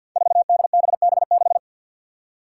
Morse code, heard as a keyed tone, about 47 wpm, 700 Hz. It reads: 4DBB6
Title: Five Challenging Letter-Number Combinations 50wpm